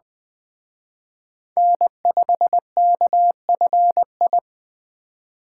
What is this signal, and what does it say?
Morse code, heard as a keyed tone, about 21 wpm, 700 Hz. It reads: N5KFI